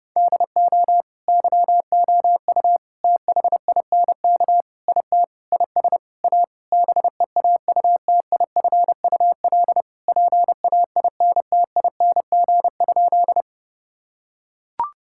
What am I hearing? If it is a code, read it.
Morse code, heard as a keyed tone, about 30 wpm, 700 Hz. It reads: DO YOU THINK IT IS A BEAUTIFUL PAINTING?
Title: Sentences from Top 1000 Words - Review Part 6 - 30wpm